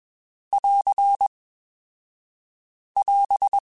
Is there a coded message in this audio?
Yes: Morse code from 0.5 to 3.6 s, heard as a keyed tone, about 22 wpm, 2550 Hz. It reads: <AR> <AS>